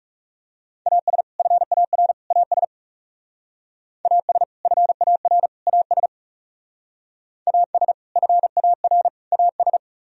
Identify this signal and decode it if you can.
Morse code, heard as a keyed tone, about 40 wpm, 700 Hz. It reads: AS FAR AS AS FAR AS AS FAR AS